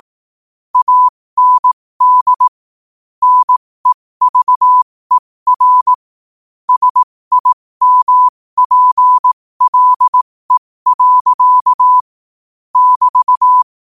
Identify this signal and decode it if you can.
Morse code, heard as a keyed tone, about 18 wpm, spaced as for 15 wpm, 1000 Hz. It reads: AND NEVER SIMPLE. <BT>
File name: qotdfiles / 15wpm